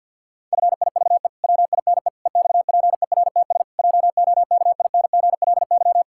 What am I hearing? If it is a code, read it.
Morse code, heard as a keyed tone, about 50 wpm, 700 Hz. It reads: FIVE WINE EXPERTS JOKINGLY
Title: Pangram Challenge v2 - 50wpm